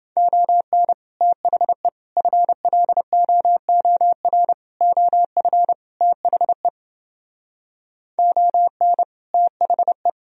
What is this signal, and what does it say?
Morse code, heard as a keyed tone, about 29 wpm, 700 Hz. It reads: ON THE FLOOR OF THE ON THE